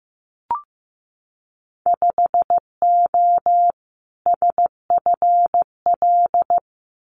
Morse code, heard as a keyed tone, about 15 wpm, 700 Hz. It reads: 5O SFL
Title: Winter Field Day Exchanges 15wpm